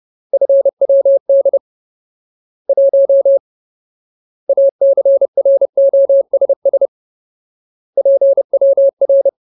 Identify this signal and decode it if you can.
Morse code, heard as a keyed tone, about 30 wpm, 550 Hz. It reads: FWD 1 ACROSS PWR
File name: Ultimate Word List / 30WPM Ultimate